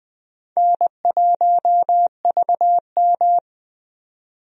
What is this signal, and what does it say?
Morse code, heard as a keyed tone, about 20 wpm, 700 Hz. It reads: N1VM
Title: USA Call Signs - 4 Characters - 20wpm